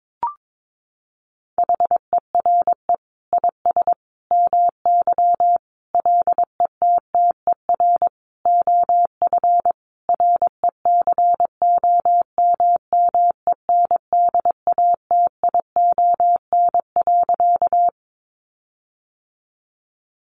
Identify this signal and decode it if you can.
Morse code, heard as a keyed tone, about 22 wpm, 700 Hz. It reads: HERE IS MY LETTER OF RECOMMENDATION.
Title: Sentences from Top 1400 Words 22wpm